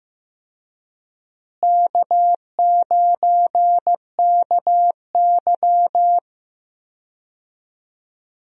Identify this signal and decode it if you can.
Morse code, heard as a keyed tone, about 16 wpm, 700 Hz. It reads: K9KY
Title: USA Call Signs - 4 Characters - Repeated 2x - 15wpm